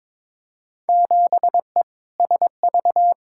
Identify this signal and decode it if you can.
Morse code, heard as a keyed tone, about 22 wpm, 700 Hz. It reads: 7E SV
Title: Field Day Exchanges Encore - 22wpm